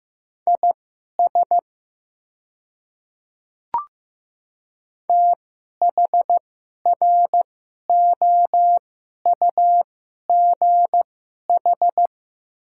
Morse code, heard as a keyed tone, about 15 wpm, spaced as for 11 wpm, 690 Hz. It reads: IS THROUGH
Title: 2x Character Spacing - Top 200 Words - 15wpm